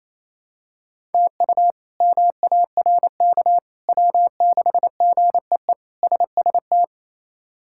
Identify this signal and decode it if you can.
Morse code, heard as a keyed tone, about 27 wpm, 700 Hz. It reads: TU MARK W6GEE SST
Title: K1USN SST QSO Practice - 28wpm